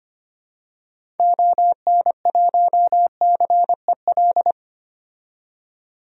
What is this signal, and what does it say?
Morse code, heard as a keyed tone, about 25 wpm, 700 Hz. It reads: ON1CEL
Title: DXCC-100 Call Signs - 6 Characters - 25wpm